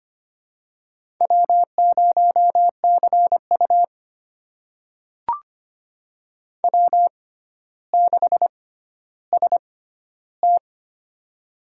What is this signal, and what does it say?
Morse code, heard as a keyed tone, about 25 wpm, spaced as for 13 wpm, 700 Hz. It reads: W0CU W6ST
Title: USA Call Signs - 4 Characters - 6x Character Spacing - 25wpm